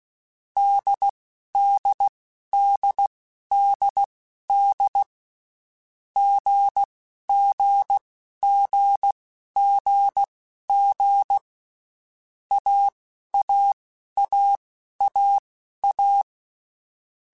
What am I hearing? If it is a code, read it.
Morse code, heard as a keyed tone, about 16 wpm, spaced as for 11 wpm, 800 Hz. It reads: DDDDD GGGGG AAAAA
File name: Basic Letter Groups By7Groups / AWUNDG / AWUNDG@10